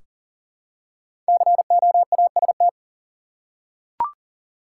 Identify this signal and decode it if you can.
Morse code, heard as a keyed tone, about 39 wpm, 700 Hz. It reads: COAST
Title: Five Letter Words - 40wpm